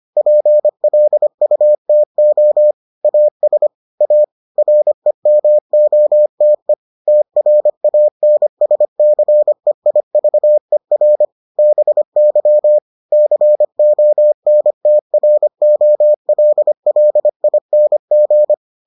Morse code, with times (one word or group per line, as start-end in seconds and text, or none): 0.2-2.7 PLUTO
3.0-3.7 AS
4.0-4.2 A
4.6-6.7 REMOTE
7.1-11.3 TRANSCEIVER
11.6-12.8 BY
13.1-18.5 CONTROLLING